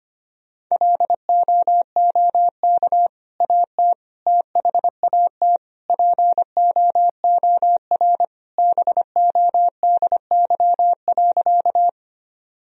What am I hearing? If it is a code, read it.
Morse code, heard as a keyed tone, about 25 wpm, 700 Hz. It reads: LOOK AT THAT POOR BODY.
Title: Sentences from Top 1000 Words - Review Part 2 - 25wpm